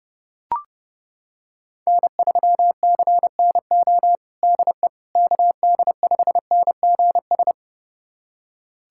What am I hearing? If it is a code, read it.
Morse code, heard as a keyed tone, about 30 wpm, 700 Hz. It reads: N3CNO DE KD5NGS
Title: USA Callsign DE Callsign 30wpm